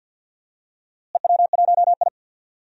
Morse code, heard as a keyed tone, about 52 wpm, 700 Hz. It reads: EW1I
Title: DXCC-100 Call Signs - 4 Characters - Repeated 3x - 50wpm